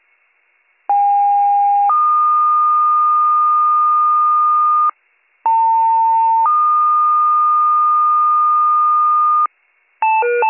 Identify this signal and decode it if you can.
Two-tone page, heard as a one-second tone, then a three-second tone, about 810 then 1220 Hz, about 900 then 1220 Hz. It is heard from 0.9 to 9.5 s.